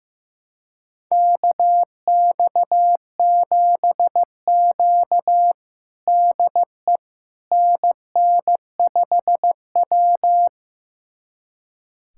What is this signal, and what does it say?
Morse code, heard as a keyed tone, about 15 wpm, 700 Hz. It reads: KX7Q DE NN5W
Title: USA Callsign DE Callsign - 4 Characters - 15wpm